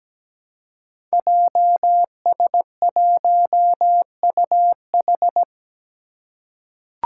Morse code, heard as a keyed tone, about 17 wpm, 700 Hz. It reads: JS1UH
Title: Five Challenging Letter-Number Combinations 17wpm